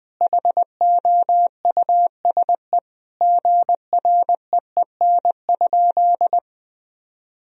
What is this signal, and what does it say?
Morse code, heard as a keyed tone, about 20 wpm, 700 Hz. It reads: HOUSE GREEN?
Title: Sentences from Top 1000 Words - Review Part 2 - 20wpm